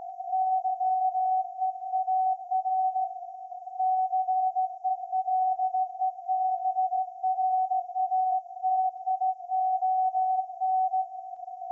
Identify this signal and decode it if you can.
Morse code, heard as a keyed tone, about 16 wpm, 750 Hz. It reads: YEAR CELEBRATION
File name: iz7slz / VLF / SAQ 30-06-2024 0900